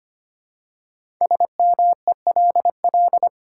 Morse code, heard as a keyed tone, about 25 wpm, 700 Hz. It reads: SMELL
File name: 5 Letter Words - First 3 Letters Given - 25wpm